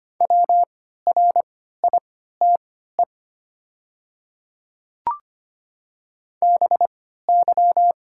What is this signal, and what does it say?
Morse code, heard as a keyed tone, about 25 wpm, spaced as for 14 wpm, 700 Hz. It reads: WRITE BY
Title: 3x Character Spacing - Top 500 Words - 25wpm